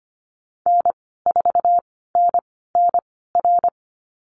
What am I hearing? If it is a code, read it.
Morse code, heard as a keyed tone, about 25 wpm, spaced as for 16 wpm, 700 Hz. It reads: N4NNR
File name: Lesson 15g - Number 5 - Call Signs 25wpm